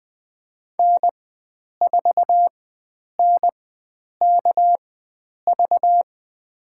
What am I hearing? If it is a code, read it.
Morse code, heard as a keyed tone, about 20 wpm, spaced as for 9 wpm, 700 Hz. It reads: N4NKV
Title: USA Call Signs - 4x Character Spacing - 20wpm